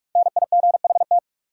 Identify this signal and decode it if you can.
Morse code, heard as a keyed tone, about 49 wpm, 700 Hz. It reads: NIGHT